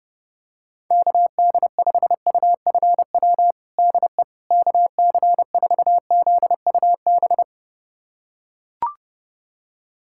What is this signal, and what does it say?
Morse code, heard as a keyed tone, about 32 wpm, 700 Hz. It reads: KD5UFW DE KC4ZUB
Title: USA Callsign DE Callsign - 6 Characters - 30wpm